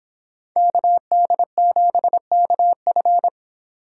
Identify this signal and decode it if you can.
Morse code, heard as a keyed tone, about 30 wpm, 700 Hz. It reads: KD7KF